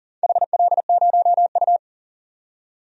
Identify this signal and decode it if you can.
Morse code, heard as a keyed tone, about 40 wpm, 700 Hz. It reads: HL0U
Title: DXCC-100 Call Signs - 4 Characters - 40wpm